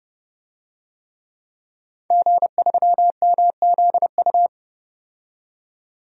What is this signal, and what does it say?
Morse code, heard as a keyed tone, about 30 wpm, 700 Hz. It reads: G3MZU
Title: DXCC-100 Call Signs - 5 Characters - 30wpm